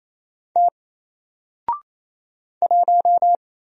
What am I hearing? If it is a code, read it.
Morse code, heard as a keyed tone, about 28 wpm, 700 Hz. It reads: T 1
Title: Single Letter-Number 28wpm